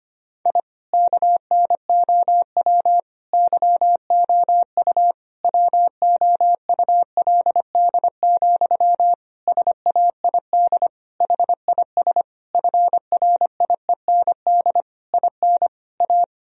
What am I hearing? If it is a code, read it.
Morse code, heard as a keyed tone, about 25 wpm, 700 Hz. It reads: I KNOW YOU WOULD, SAID HIS FRIEND IN A
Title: Sentences from Top 1000 Words - Review Part 1 - 25wpm